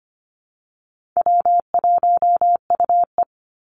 Morse code, heard as a keyed tone, about 25 wpm, 700 Hz. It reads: W1UE
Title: Easy Call Signs Repeated 3x - 25wpm